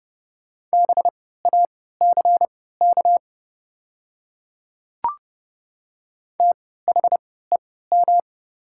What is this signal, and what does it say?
Morse code, heard as a keyed tone, about 30 wpm, spaced as for 17 wpm, 690 Hz. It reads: BACK THEM
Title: 3x Character Spacing - Top 200 Words - 30wpm